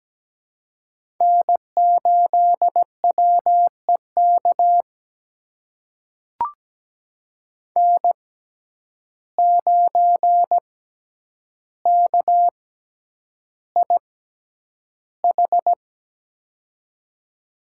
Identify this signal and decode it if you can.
Morse code, heard as a keyed tone, about 17 wpm, spaced as for 9 wpm, 700 Hz. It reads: N8WEK N9KIH
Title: USA Call Signs - 5 Characters - 6x Character Spacing - 17wpm